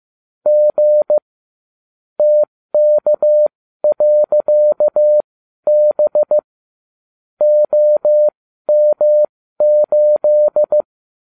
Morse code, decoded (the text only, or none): G TK.B OM8